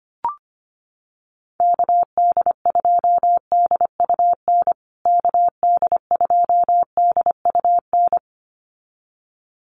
Morse code, heard as a keyed tone, about 25 wpm, 700 Hz. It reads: KD2DUN KD2DUN
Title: USA Call Signs Repeated 2x 25wpm